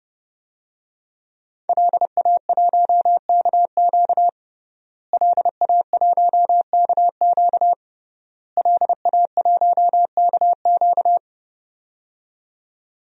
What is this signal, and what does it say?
Morse code, heard as a keyed tone, about 31 wpm, 700 Hz. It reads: LA1KQ LA1KQ LA1KQ
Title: DXCC-100 Call Signs - 5 Characters - Repeated 3x - 30wpm